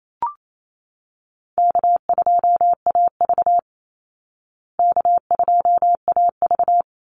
Morse code, heard as a keyed tone, about 28 wpm, 700 Hz. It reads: K2AV K2AV
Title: Easy Call Signs Repeated 2x - 28wpm